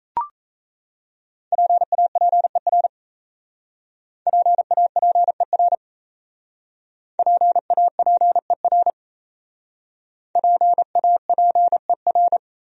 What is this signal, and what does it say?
Morse code, heard as a keyed tone, about 35 wpm, 700 Hz. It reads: PAPER PAPER PAPER PAPER
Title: Speed-Racing Top 300 Words 28wpm